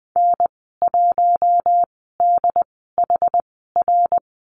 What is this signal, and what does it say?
Morse code, heard as a keyed tone, about 20 wpm, spaced as for 15 wpm, 700 Hz. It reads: N1DHR
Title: 167 Lesson 27g - Number 7 - Call Signs 20wpm